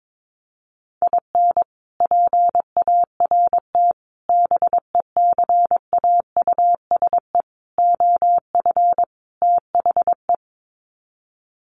Morse code, heard as a keyed tone, about 22 wpm, 700 Hz. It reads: IN PART BECAUSE OF THE
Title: Sets of 5 Words Encore2 22wpm